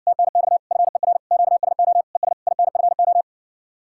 Morse code, heard as a keyed tone, about 59 wpm, 700 Hz. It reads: TNX FER QSO ES INFO